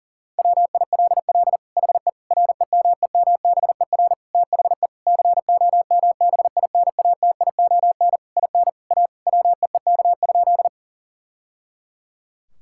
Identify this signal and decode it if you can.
Morse code, heard as a keyed tone, about 40 wpm, 700 Hz. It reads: WILL HE REMEMBER THE COMBINATION IN A WEEK?
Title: Sentences from Top 1000 Words - Review Part 6 - 40wpm